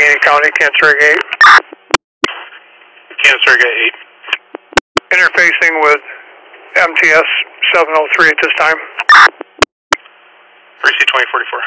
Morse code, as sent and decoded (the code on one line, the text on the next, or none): none